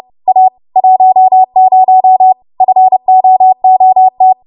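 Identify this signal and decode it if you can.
Morse code, heard as a keyed tone, about 30 wpm, 750 Hz. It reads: A 10 FOOT
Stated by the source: Morse practice transmission at 30 WPM.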